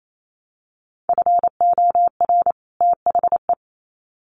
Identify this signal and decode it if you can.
Morse code, heard as a keyed tone, about 28 wpm, 700 Hz. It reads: FOR THE
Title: Sets of 2 Words 28wpm